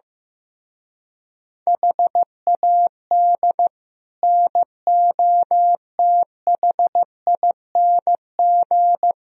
Morse code, heard as a keyed tone, about 15 wpm, 700 Hz. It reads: HAD NOTHING
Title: Speed-Racing - Sets of 5 Words 15wpm